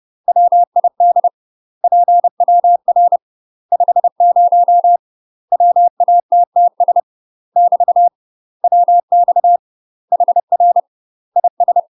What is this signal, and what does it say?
Morse code, heard as a keyed tone, about 30 wpm, 700 Hz. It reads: WID PWR 50 WATTS <BT> WX HR IS
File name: UK MASTER QSOs TRAINING / 30WPM / UK MASTER QSOs 0005